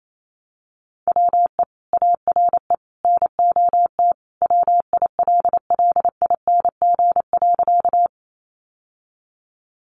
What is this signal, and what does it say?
Morse code, heard as a keyed tone, about 28 wpm, 700 Hz. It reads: WE ARE NOT WILLING.
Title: Sentences from Top 1000 Words 28wpm